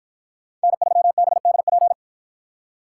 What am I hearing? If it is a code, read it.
Morse code, heard as a keyed tone, about 48 wpm, 700 Hz. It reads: N3BDP